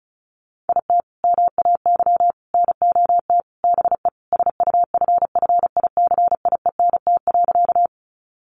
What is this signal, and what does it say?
Morse code, heard as a keyed tone, about 34 wpm, 700 Hz. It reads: IT MAY NOT BE SUFFICIENT.
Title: Sentences from Top 1400 Words 35wpm